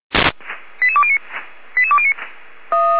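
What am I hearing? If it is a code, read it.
Five-tone selective call (ZVEI), heard as quick runs of five tones, tones of about 70 ms. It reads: 88209 88209
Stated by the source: ZVEI five-tone selective call received on SDR.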